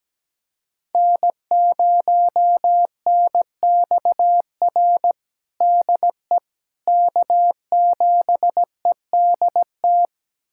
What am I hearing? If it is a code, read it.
Morse code, heard as a keyed tone, about 17 wpm, 700 Hz. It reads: N0NXR DE K7EDT
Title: USA Callsign DE Callsign - 5 Characters - 17wpm